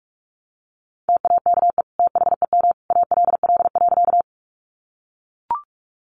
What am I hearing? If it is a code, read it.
Morse code, heard as a keyed tone, about 44 wpm, 700 Hz. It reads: TAKE THEM ALL.
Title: Sentences from Top 200 Words 45wpm